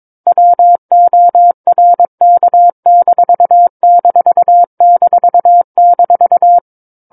Morse code, heard as a keyed tone, about 22 wpm, 700 Hz. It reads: WORK----
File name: Book One / 13-Chapter 13 - 22 WPM